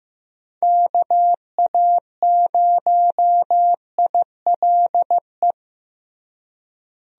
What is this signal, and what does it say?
Morse code, heard as a keyed tone, about 15 wpm, 700 Hz. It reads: KA0ILE
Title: USA Call Signs - 6 Characters - Repeated 2x - 15wpm